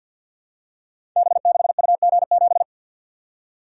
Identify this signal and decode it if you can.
Morse code, heard as a keyed tone, about 52 wpm, 700 Hz. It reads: B6UG7
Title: Code Groups - 5 Letter:Numbers - 50wpm